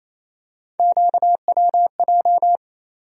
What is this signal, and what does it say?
Morse code, heard as a keyed tone, about 28 wpm, 700 Hz. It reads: QWJ
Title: USA Call Sign Postfixes - 3 Letters 28wpm